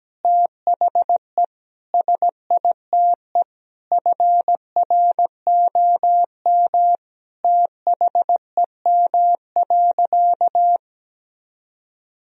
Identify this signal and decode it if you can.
Morse code, heard as a keyed tone, about 17 wpm, 700 Hz. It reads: THE SITE FROM THEM.
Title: Sentences from Top 1000 Words - Review Part 4 - 17wpm